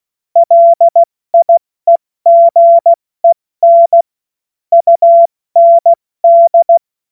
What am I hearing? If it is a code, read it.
Morse code, heard as a keyed tone, about 16 wpm, 680 Hz. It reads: LIEGEN UND
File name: Morsekurs / Witze / Witz 17. August 2021, 70 F80